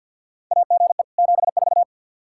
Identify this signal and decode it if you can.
Morse code, heard as a keyed tone, about 52 wpm, 700 Hz. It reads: AGE 74
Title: Practice QSO - Exchange 3 - More Info - 50wpm